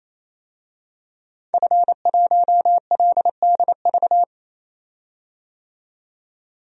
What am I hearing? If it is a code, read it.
Morse code, heard as a keyed tone, about 28 wpm, 700 Hz. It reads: F1LDV
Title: DXCC-100 Call Signs - 5 Characters - 28wpm